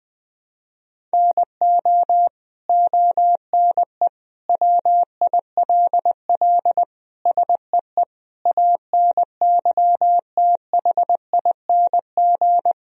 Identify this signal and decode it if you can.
Morse code, heard as a keyed tone, about 21 wpm, 700 Hz. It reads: NO ONE WILL SEE ANYTHING
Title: Sentences from Top 1000 Words - Review Part 3 - 20wpm